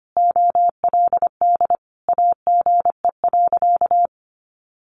Morse code, heard as a keyed tone, about 25 wpm, 700 Hz. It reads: OLD AGE.
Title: Sentences from Top 1100 Words 25wpm